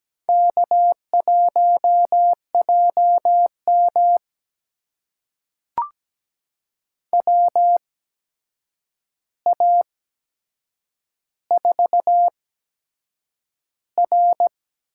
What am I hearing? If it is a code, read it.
Morse code, heard as a keyed tone, about 17 wpm, spaced as for 7 wpm, 700 Hz. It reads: K1JM WA4R
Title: USA Call Signs - 8x Character Spacing - 17wpm